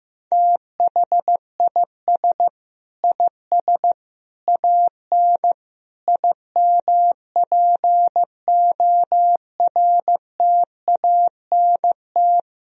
Morse code, heard as a keyed tone, about 15 wpm, 700 Hz. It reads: THIS IS AN IMPORTANT
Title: Sentences from Top 1000 Words - Review Part 2 - 15wpm